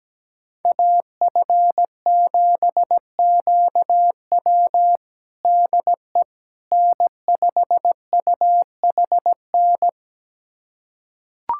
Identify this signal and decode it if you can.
Morse code, heard as a keyed tone, about 17 wpm, 700 Hz. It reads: AF7QW DE N5UHN
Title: USA Callsign DE Callsign - 5 Characters - 17wpm